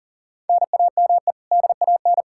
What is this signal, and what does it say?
Morse code, heard as a keyed tone, about 40 wpm, 700 Hz. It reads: NAME DAN
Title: Practice QSO - Exchange 1 - Beginning - 40wpm